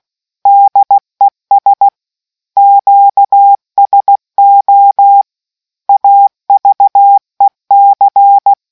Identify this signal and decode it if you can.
Morse code, heard as a keyed tone, about 16 wpm, 800 Hz. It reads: DES QSO AVEC